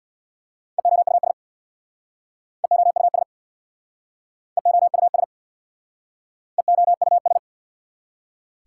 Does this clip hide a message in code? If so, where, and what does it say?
Yes: Morse code from 0.8 to 7.4 s, heard as a keyed tone, about 64 wpm, 700 Hz. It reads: EOUS EOUS EOUS EOUS